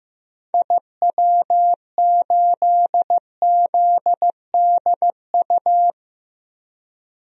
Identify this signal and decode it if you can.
Morse code, heard as a keyed tone, about 15 wpm, 700 Hz. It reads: IW8ZDU
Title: DXCC-100 Call Signs - 6 Characters - 15wpm